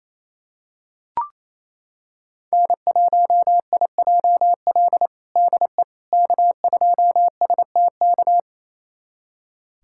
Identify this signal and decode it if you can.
Morse code, heard as a keyed tone, about 28 wpm, 700 Hz. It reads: N1IJL DE K2STK
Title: USA Callsign DE Callsign - 5 Characters - 28wpm